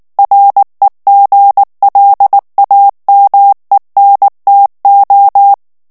Morse code, heard as a keyed tone, about 19 wpm, 800 Hz. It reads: REGLAMENTO
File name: morse mp3 / ESPAGNOL TEXTE EN CLAIR 2 VITESSE 18 MOTS MINUTES